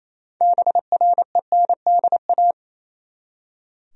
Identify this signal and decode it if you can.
Morse code, heard as a keyed tone, about 28 wpm, 700 Hz. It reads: BRENDA
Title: USA Common Names - 28wpm